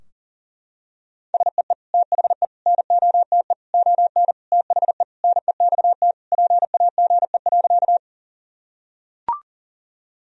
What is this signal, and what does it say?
Morse code, heard as a keyed tone, about 40 wpm, 700 Hz. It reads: SEE THE NOTE ON THE NEXT PAGE.
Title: Sentences from Top 1000 Words - Review Part 2 - 40wpm